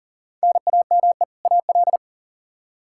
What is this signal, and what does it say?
Morse code, heard as a keyed tone, about 40 wpm, 700 Hz. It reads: NAME AL
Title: K1USN SST QSO Practice 40wpm